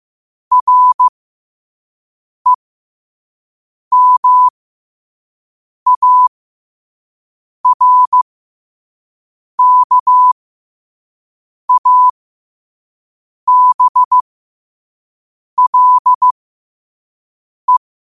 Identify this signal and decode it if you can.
Morse code, heard as a keyed tone, about 15 wpm, spaced as for 5 wpm, 1000 Hz. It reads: REMARKABLE
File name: qotdfiles / 5wpm